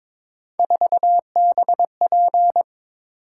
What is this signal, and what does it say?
Morse code, heard as a keyed tone, about 26 wpm, 700 Hz. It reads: 4BP
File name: Speed-Racing - Three Letter-Number Combinations - 22wpm